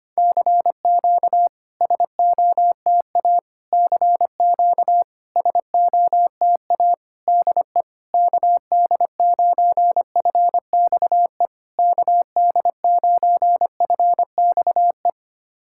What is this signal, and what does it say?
Morse code, heard as a keyed tone, about 25 wpm, 700 Hz. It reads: CQ SOTA CQ SOTA DE KD9FXE KD9FXE
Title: SOTA QSO Practice 25wpm